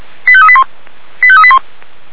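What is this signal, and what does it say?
Five-tone selective call (ZVEI), heard as quick runs of five tones, tones of about 70 ms. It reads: 85381 85381
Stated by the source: ZVEI five-tone selective call received on an SDR.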